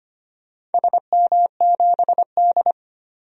Morse code, heard as a keyed tone, about 25 wpm, 700 Hz. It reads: SM7D
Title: DXCC-100 Call Signs - 4 Characters - 25wpm